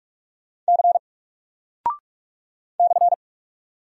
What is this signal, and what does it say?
Morse code, heard as a keyed tone, about 44 wpm, 700 Hz. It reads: C /
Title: Single Character (Letter:Number:Punctuation:Prosign) Alt 45wpm